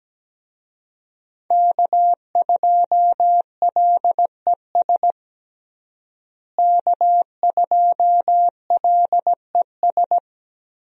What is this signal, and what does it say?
Morse code, heard as a keyed tone, about 17 wpm, 700 Hz. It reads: K2LES K2LES
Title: USA Call Signs - 5 Characters - Repeated 2x - 17wpm